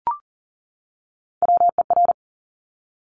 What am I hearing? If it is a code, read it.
Morse code, heard as a keyed tone, about 42 wpm, 700 Hz. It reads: WER
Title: Top Three Letter Combinations in Words 40wpm